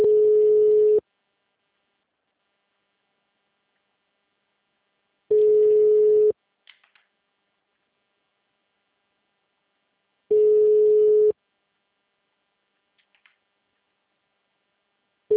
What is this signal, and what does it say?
Ringback tone (Europe, 425 Hz), with speech.
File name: out- -2011- - - .260